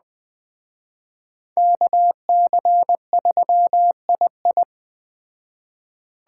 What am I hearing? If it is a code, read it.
Morse code, heard as a keyed tone, about 20 wpm, 700 Hz. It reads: KC3II
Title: Speed Racing - USA 2x2 Call Signs 17wpm